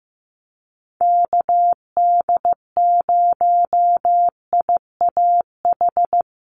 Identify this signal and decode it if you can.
Morse code, heard as a keyed tone, about 15 wpm, 700 Hz. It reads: KD0IAH
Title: USA Call Signs Repeated 2x 15wpm